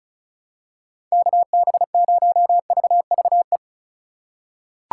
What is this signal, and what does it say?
Morse code, heard as a keyed tone, about 36 wpm, 700 Hz. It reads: KB0VVE
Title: USA Call Signs - 6 Characters - 2x Character Spacing - 35wpm